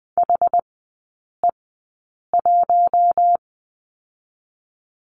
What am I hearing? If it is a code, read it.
Morse code, heard as a keyed tone, about 19 wpm, spaced as for 8 wpm, 700 Hz. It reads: HE1
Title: 067 Lesson 12e - Letter d - Three Characters 20wpm